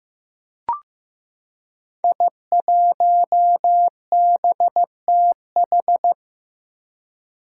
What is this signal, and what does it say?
Morse code, heard as a keyed tone, about 15 wpm, 700 Hz. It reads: I1BTH
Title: Five Challenging Letter-Number Combinations 15wpm